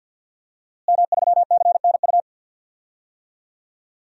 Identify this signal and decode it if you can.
Morse code, heard as a keyed tone, about 46 wpm, 700 Hz. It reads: M3CNU
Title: DXCC-100 Call Signs - 5 Characters - Repeated 3x - 50wpm